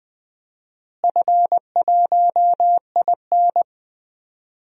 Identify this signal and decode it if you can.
Morse code, heard as a keyed tone, about 20 wpm, 700 Hz. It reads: F1IN
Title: DXCC-100 Call Signs - 4 Characters - Repeated 3x - 20wpm